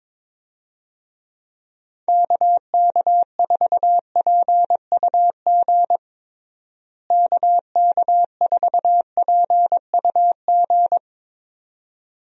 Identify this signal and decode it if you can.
Morse code, heard as a keyed tone, about 21 wpm, 700 Hz. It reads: KK4PUG KK4PUG
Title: USA Call Signs - 6 Characters - Repeated 2x - 22wpm